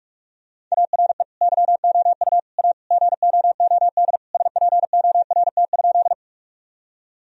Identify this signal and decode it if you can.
Morse code, heard as a keyed tone, about 46 wpm, 700 Hz. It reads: ARE YOU A GOOD SPORT?
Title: Sentences from Top 1000 Words - Review Part 4 - 45wpm